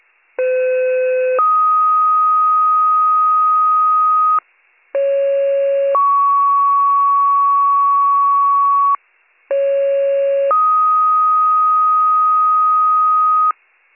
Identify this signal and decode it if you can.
Two-tone page, heard as a one-second tone, then a three-second tone, about 510 then 1250 Hz, about 550 then 1120 Hz, about 550 then 1250 Hz. It is heard from 0.4 to 13.5 s.